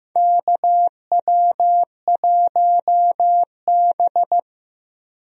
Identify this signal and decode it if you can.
Morse code, heard as a keyed tone, about 15 wpm, 700 Hz. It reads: KW1B
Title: USA Call Signs - 4 Characters - Repeated 3x - 15wpm